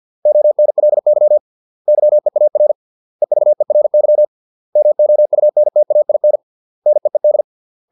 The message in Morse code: ONLY YEAR EVERY MOUNTAIN NEED

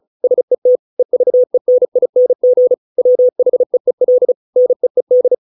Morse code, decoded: SET EVENING WHEEL NEED

